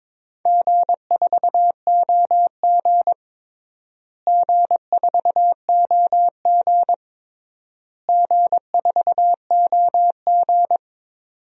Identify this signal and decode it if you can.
Morse code, heard as a keyed tone, about 22 wpm, 700 Hz. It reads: G4OG G4OG G4OG
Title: DXCC-100 Call Signs - 4 Characters - Repeated 3x - 22wpm